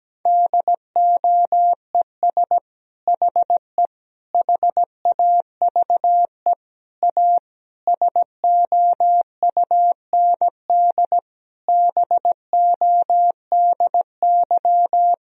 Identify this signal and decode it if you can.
Morse code, heard as a keyed tone, about 17 wpm, 700 Hz. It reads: DOES HE HAVE A SOUND BODY
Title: Sentences from Top 1000 Words - Review Part 2 - 17wpm